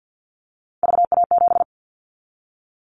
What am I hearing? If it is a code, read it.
Morse code, heard as a keyed tone, about 50 wpm, 700 Hz. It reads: VA7
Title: Callsign Prefixes for Top 100 DXCC - 50wpm